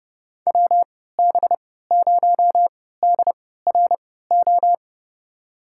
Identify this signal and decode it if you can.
Morse code, heard as a keyed tone, about 32 wpm, spaced as for 17 wpm, 700 Hz. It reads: WB0DRO
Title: USA Call Signs - 6 Characters - 3x Character Spacing - 30wpm